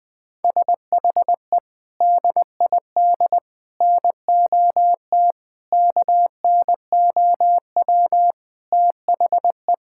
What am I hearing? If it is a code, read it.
Morse code, heard as a keyed tone, about 20 wpm, 700 Hz. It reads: SHE DID NOT KNOW THE
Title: Sentences from Top 1000 Words - Review Part 6 - 20wpm